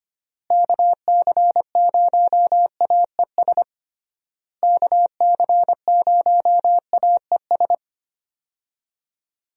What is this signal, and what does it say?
Morse code, heard as a keyed tone, about 25 wpm, 700 Hz. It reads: KC0AES KC0AES
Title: USA Call Signs - 6 Characters - Repeated 3x - 25wpm